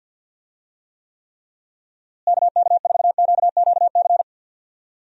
Morse code, heard as a keyed tone, about 46 wpm, 700 Hz. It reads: KK4QQC
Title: USA Call Signs - 6 Characters - Repeated 3x - 50wpm